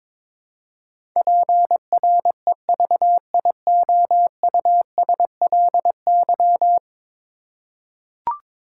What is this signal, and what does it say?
Morse code, heard as a keyed tone, about 22 wpm, 700 Hz. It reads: PREVIOUSLY
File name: practice / 22wpm / 10 Letter Words - First 3 Letters Given - 22wpm